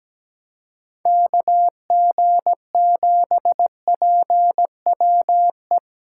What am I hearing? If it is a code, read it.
Morse code, heard as a keyed tone, about 17 wpm, 700 Hz. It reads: KG7PWE
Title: USA Callsign DE Callsign - 6 Characters - 17wpm